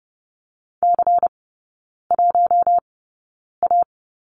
Morse code, heard as a keyed tone, about 30 wpm, spaced as for 9 wpm, 700 Hz. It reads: C1A
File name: Lesson 16e - Letter c - Three Characters 30wpm